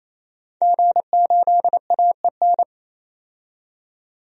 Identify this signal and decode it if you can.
Morse code, heard as a keyed tone, about 28 wpm, 700 Hz. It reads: G8AEN
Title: DXCC-100 Call Signs - 5 Characters - 28wpm